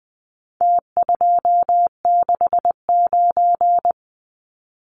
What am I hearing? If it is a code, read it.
Morse code, heard as a keyed tone, about 20 wpm, 700 Hz. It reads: T269
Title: Numbers - Contest Style - 4-Digit 20wpm